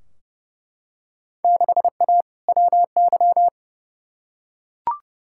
Morse code, heard as a keyed tone, about 30 wpm, 700 Hz. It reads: 6A WY
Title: Field Day Exchanges Encore - 30wpm